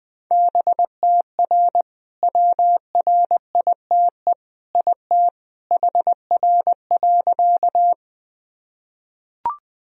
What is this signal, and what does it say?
Morse code, heard as a keyed tone, about 21 wpm, 700 Hz. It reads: BTR WRITE IT HR.
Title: Sentences from Top 400 Words and Abbreviations 20wpm